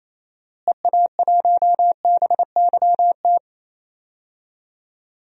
Morse code, heard as a keyed tone, about 28 wpm, 700 Hz. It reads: EA1BYT
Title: DXCC-100 Call Signs - 6 Characters - Repeated 3x - 28wpm